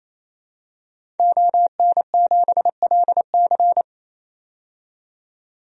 Morse code, heard as a keyed tone, about 28 wpm, 700 Hz. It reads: ON7LC
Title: DXCC-100 Call Signs - 5 Characters - 28wpm